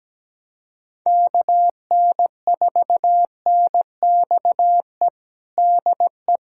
Morse code, heard as a keyed tone, about 17 wpm, 700 Hz. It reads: KN4NXE DE
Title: USA Callsign DE Callsign 17wpm